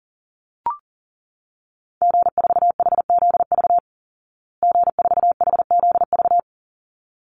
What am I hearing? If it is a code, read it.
Morse code, heard as a keyed tone, about 40 wpm, 700 Hz. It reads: G4HZV G4HZV
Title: Call Signs Repeated 3x - 40wpm